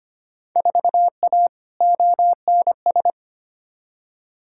Morse code, heard as a keyed tone, about 25 wpm, 700 Hz. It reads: 4A ONS
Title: Field Day Exchanges - 25wpm